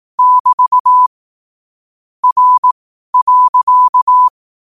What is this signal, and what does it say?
Morse code, heard as a keyed tone, about 18 wpm, spaced as for 12 wpm, 1000 Hz. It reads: <BT> R.
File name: qotdfiles / 12wpm